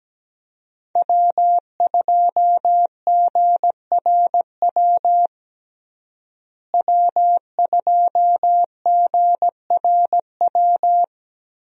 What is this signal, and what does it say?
Morse code, heard as a keyed tone, about 17 wpm, 700 Hz. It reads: W2GRW W2GRW
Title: USA Call Signs - 5 Characters - Repeated 3x - 17wpm